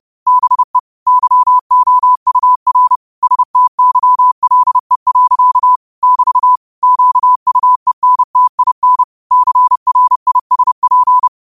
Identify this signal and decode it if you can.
Morse code, heard as a keyed tone, about 30 wpm, 1000 Hz. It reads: BE YOUR STYLE. <BT> QUENTIN CRISP